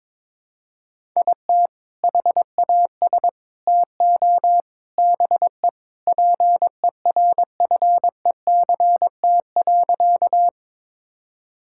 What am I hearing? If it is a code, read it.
Morse code, heard as a keyed tone, about 22 wpm, 700 Hz. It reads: IT HAS TO BE PERFECT.
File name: practice / 22wpm / Sentences with 3-Word Phrases - 22wpm